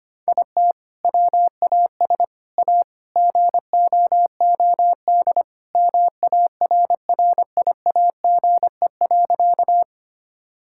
Morse code, heard as a keyed tone, about 25 wpm, 700 Hz. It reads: IT WAS A GOOD MARRIAGE.
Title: Sentences from Top 1000 Words - Review Part 5 - 25wpm